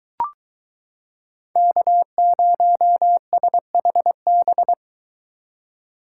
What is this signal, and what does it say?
Morse code, heard as a keyed tone, about 22 wpm, 700 Hz. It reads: K0SHB